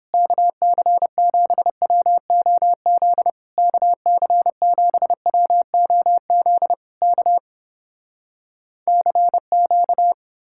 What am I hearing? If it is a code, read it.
Morse code, heard as a keyed tone, about 29 wpm, 700 Hz. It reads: KC7WOZ KC7WOZ K CQ